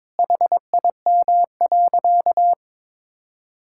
Morse code, heard as a keyed tone, about 22 wpm, 700 Hz. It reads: HIM.
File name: practice / 22wpm / Sentences with 3-Word Phrases - 22wpm